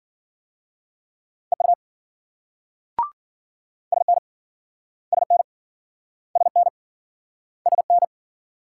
Morse code, heard as a keyed tone, about 53 wpm, 700 Hz. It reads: EU SN SN SN SN